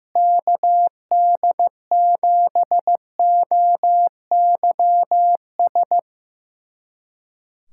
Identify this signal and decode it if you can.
Morse code, heard as a keyed tone, about 15 wpm, 700 Hz. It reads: KD7OYS
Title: USA Callsign DE Callsign - 6 Characters - 15wpm